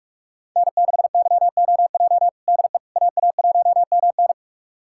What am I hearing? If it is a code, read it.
Morse code, heard as a keyed tone, about 46 wpm, 700 Hz. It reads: N6YOJ DE AA1MN
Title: USA Callsign DE Callsign - 5 Characters - 45wpm